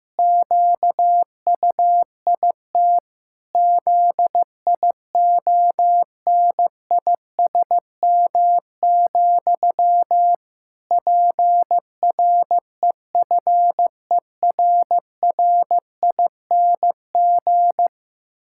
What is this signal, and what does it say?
Morse code, heard as a keyed tone, about 15 wpm, 700 Hz. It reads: QUIT ZIONISM, PREFERRING
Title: Pangram Challenge v2 - 15wpm